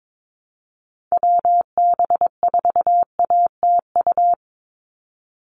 Morse code, heard as a keyed tone, about 22 wpm, 700 Hz. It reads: WB4ATU
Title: USA Call Signs 22wpm